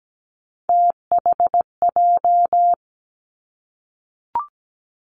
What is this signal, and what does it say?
Morse code, heard as a keyed tone, about 17 wpm, 700 Hz. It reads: THJ
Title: Three Challenging Letter-Number Combinations 17wpm